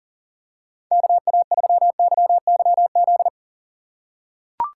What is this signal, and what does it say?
Morse code, heard as a keyed tone, about 39 wpm, 690 Hz. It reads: KA3YYZ
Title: USA Call Signs - 4x Character Spacing - 40wpm